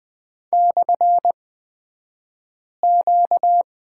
Morse code, heard as a keyed tone, about 20 wpm, spaced as for 5 wpm, 700 Hz. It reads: /Q
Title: 235 Lesson 38d - Letter q - Two Characters 20wpm